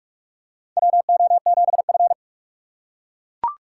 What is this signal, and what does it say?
Morse code, heard as a keyed tone, about 42 wpm, 700 Hz. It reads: WO7F
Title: USA Call Signs - 4 Characters - 3x Character Spacing - 45wpm